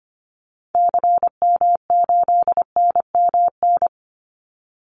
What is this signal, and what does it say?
Morse code, heard as a keyed tone, about 25 wpm, 700 Hz. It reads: CM8NMN
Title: Call Signs Repeated 2x - 25wpm